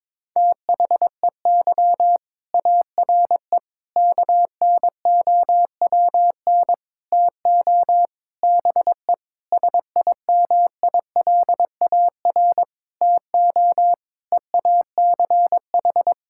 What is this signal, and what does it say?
Morse code, heard as a keyed tone, about 22 wpm, 700 Hz. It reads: THEY ARE KNOWN TO BE SIMILAR TO EACH
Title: Sentences from Top 1000 Words - Review Part 5 - 22wpm